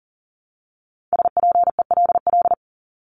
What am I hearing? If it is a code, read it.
Morse code, heard as a keyed tone, about 40 wpm, 700 Hz. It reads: SPELL
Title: Top 500 Words Repeated 40wpm